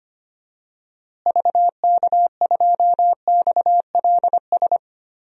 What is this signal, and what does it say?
Morse code, heard as a keyed tone, about 25 wpm, 700 Hz. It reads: VK2XLS
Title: DXCC-100 Call Signs - 6 Characters - Repeated 3x - 25wpm